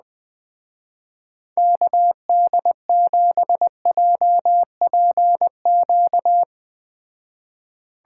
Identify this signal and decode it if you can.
Morse code, heard as a keyed tone, about 20 wpm, 700 Hz. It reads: KD7JPQ
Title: Speed Racing - USA 2x3 Call Signs 17wpm